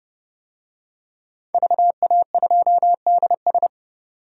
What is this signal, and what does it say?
Morse code, heard as a keyed tone, about 30 wpm, 700 Hz. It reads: VA2DS
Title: DXCC-100 Call Signs - 5 Characters - 30wpm